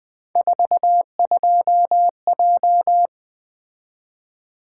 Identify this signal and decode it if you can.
Morse code, heard as a keyed tone, about 20 wpm, 700 Hz. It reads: 42J
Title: Three Letter-Number Combinations - 20wpm